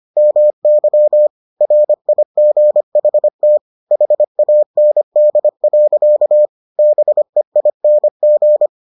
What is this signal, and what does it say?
Morse code, heard as a keyed tone, about 25 wpm, 600 Hz. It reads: MY RIGHT HAND. BEING